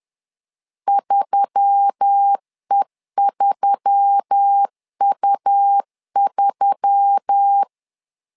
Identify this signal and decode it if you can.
Morse code, heard as a keyed tone, about 10 wpm, 800 Hz. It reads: 3E3U3
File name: LEZIONI / 4 Q1S3 10wpm 2427